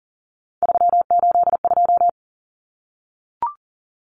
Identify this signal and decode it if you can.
Morse code, heard as a keyed tone, about 40 wpm, 700 Hz. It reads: 382
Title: Common Cut Numbers - 3-Digit 40wpm